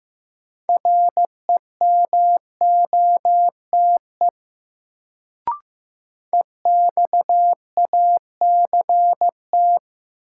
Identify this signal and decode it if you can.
Morse code, heard as a keyed tone, about 15 wpm, 700 Hz. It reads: REMOTE EXACT
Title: Sentences from Top 1300 Words - New Word Drill - 15wpm